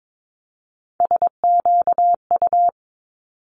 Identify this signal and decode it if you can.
Morse code, heard as a keyed tone, about 22 wpm, 700 Hz. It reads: SQU
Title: Multi-letter Phonemes 22wpm